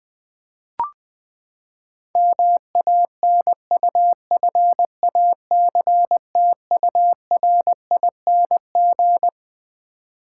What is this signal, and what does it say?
Morse code, heard as a keyed tone, about 21 wpm, 700 Hz. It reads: MANUFACTURING
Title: Practice QSO - Exchange 3 - More Info - 20wpm